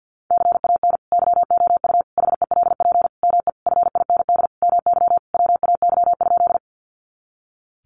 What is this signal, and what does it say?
Morse code, heard as a keyed tone, about 50 wpm, 700 Hz. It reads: CAN YOU HELP ME FIND MY WAY?
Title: Sentences from Top 200 Words 50wpm